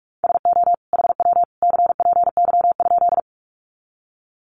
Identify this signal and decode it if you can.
Morse code, heard as a keyed tone, about 44 wpm, 700 Hz. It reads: SO HW CPY?
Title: Common QSO Phrases 45wpm